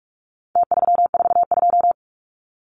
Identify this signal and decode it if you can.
Morse code, heard as a keyed tone, about 46 wpm, 700 Hz. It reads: T342
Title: Numbers - Contest Style - 4-Digit 45wpm